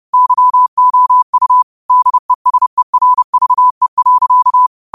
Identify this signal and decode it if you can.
Morse code, heard as a keyed tone, about 30 wpm, 1000 Hz. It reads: YOU DESERVE.